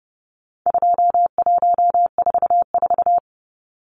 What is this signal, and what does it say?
Morse code, heard as a keyed tone, about 28 wpm, 700 Hz. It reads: 2144
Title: Numbers - 4-Digit 30wpm